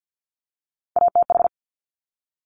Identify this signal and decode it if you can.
Morse code, heard as a keyed tone, about 50 wpm, 700 Hz. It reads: ATH
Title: Top Three Letter Combinations in Words 50wpm